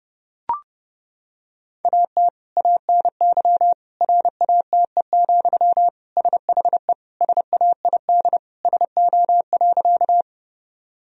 Morse code, heard as a keyed tone, about 32 wpm, 700 Hz. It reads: AT ANY RATE, SHE SAID SO.
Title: Sentences from Top 1000 Words - Review Part 4 - 30wpm